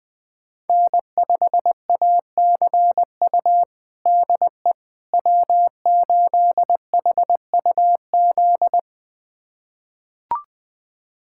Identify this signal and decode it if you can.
Morse code, heard as a keyed tone, about 21 wpm, 700 Hz. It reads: N5ACU DE W8HUZ
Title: USA Callsign DE Callsign - 5 Characters - 20wpm